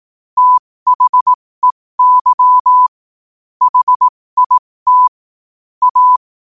Morse code, heard as a keyed tone, about 18 wpm, spaced as for 15 wpm, 1000 Hz. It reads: THEY HIT A
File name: qotdfiles / 15wpm